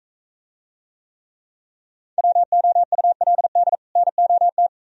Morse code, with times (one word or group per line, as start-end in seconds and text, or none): 2.2-3.8 WOULD
4.0-4.7 NOT